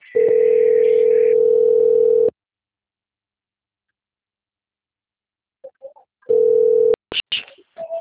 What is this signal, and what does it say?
Ringback tone (North America), 440 + 480 Hz.